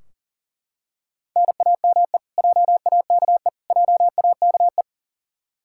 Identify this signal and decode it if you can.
Morse code, heard as a keyed tone, about 38 wpm, 700 Hz. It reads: NAME JAKE JAKE
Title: Practice QSO - Exchange 1 - Beginning - 40wpm